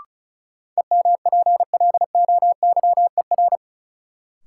Morse code, heard as a keyed tone, about 36 wpm, 700 Hz. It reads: EMPLOYER
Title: Sentences from Top 1600 Words - New Word Drill - 35wpm